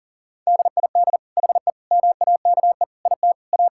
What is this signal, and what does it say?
Morse code, heard as a keyed tone, about 40 wpm, 700 Hz. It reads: DID HE MAKE IT A
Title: Sentences from Top 1600 Words 40wpm